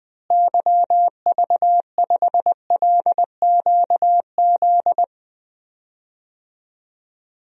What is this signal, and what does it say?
Morse code, heard as a keyed tone, about 20 wpm, 700 Hz. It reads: YV5LQZ
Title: DXCC-100 Call Signs - 6 Characters - Repeated 2x - 20wpm